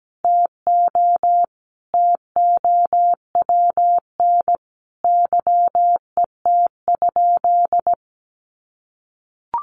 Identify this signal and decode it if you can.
Morse code, heard as a keyed tone, about 17 wpm, 700 Hz. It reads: TO TOWN YET?
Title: Sentences from Top 1500 Words 17wpm